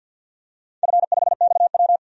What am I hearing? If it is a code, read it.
Morse code, heard as a keyed tone, about 50 wpm, 700 Hz. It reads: F5XW
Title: DXCC-100 Call Signs - 4 Characters - 50wpm